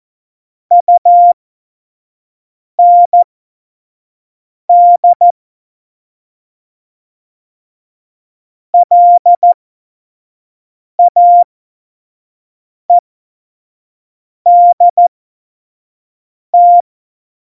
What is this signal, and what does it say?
Morse code, heard as a keyed tone, about 14 wpm, spaced as for 5 wpm, 700 Hz. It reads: UND LAEDT
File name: Morsekurs / Witze / Witz 03. November 2020, 25 F70